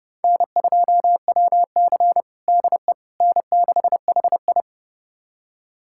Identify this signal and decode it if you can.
Morse code, heard as a keyed tone, about 32 wpm, 700 Hz. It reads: N2WC DE N6HI
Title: USA Callsign DE Callsign - 4 Characters - 30wpm